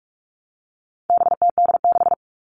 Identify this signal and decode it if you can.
Morse code, heard as a keyed tone, about 44 wpm, 700 Hz. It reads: BTD6
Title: All Cut Numbers - 4-Digit 45wpm